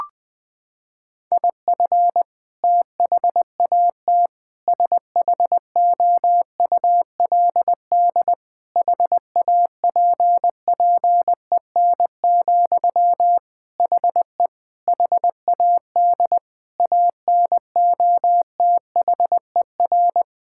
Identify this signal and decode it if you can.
Morse code, heard as a keyed tone, about 20 wpm, 700 Hz. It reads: IF THAT SHOULD HAPPEN, HE HAD ANOTHER
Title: Sentences from Top 1000 Words - Review Part 6 - 20wpm